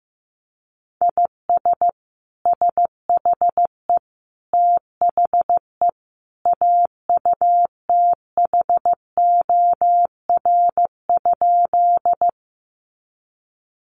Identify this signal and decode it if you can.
Morse code, heard as a keyed tone, about 15 wpm, 700 Hz. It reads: IS SHE THE AUTHOR?
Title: Sentences from Top 1200 Words 15wpm